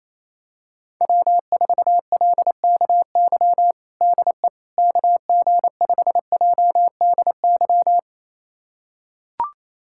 Morse code, heard as a keyed tone, about 28 wpm, 700 Hz. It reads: W4LKY DE KG5JDY
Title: USA Callsign DE Callsign 28wpm